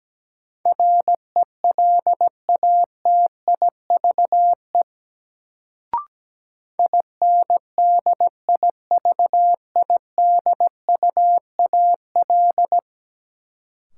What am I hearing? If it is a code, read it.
Morse code, heard as a keyed tone, about 17 wpm, 700 Hz. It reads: RELATIVE INDIVIDUAL
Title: Sentences from Top 1400 Words - New Word Drill Alt - 17wpm